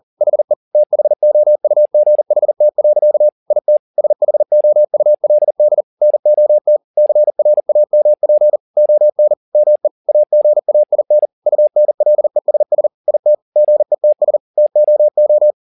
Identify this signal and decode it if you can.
Morse code, heard as a keyed tone, about 40 wpm, 600 Hz. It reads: HE THOUGHT. IT SHOULD NOT CRAMP ON ME AGAIN UNLESS IT GETS TOO
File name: the old man and the sea 2 40wpm